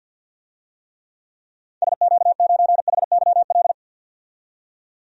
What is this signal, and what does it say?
Morse code, heard as a keyed tone, about 46 wpm, 700 Hz. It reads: SQ9HYL
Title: DXCC-100 Call Signs - 6 Characters - 50wpm